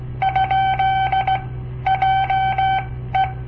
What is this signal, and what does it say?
Morse code, heard as a keyed tone, about 17 wpm, spaced as for 13 wpm, 770 Hz. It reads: ? JE
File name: papertapes / local / Code Practice-01